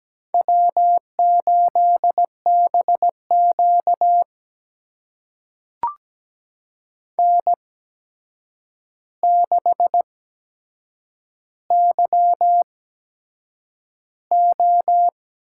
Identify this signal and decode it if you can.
Morse code, heard as a keyed tone, about 17 wpm, spaced as for 7 wpm, 700 Hz. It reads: W8BQ N6YO
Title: USA Call Signs - 4 Characters - 8x Character Spacing - 17wpm